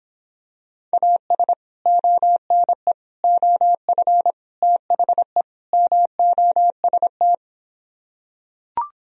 Morse code, heard as a keyed tone, about 26 wpm, 700 Hz. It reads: AS ONE OF THE MOST